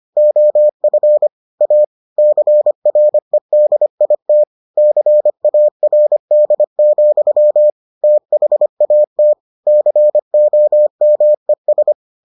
Morse code, heard as a keyed tone, about 25 wpm, 600 Hz. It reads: OF A CREDIT CARD, THAT COMES